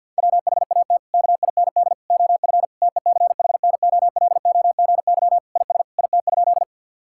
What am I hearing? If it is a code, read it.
Morse code, heard as a keyed tone, about 50 wpm, 700 Hz. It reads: WHAT KIND OF TECHNOLOGY IS IT?
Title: Sentences from Top 1000 Words - Review Part 5 - 50wpm